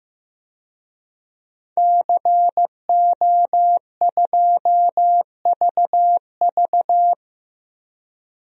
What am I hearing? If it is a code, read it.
Morse code, heard as a keyed tone, about 16 wpm, 700 Hz. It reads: CO2VV
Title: DXCC-100 Call Signs - 5 Characters - 15wpm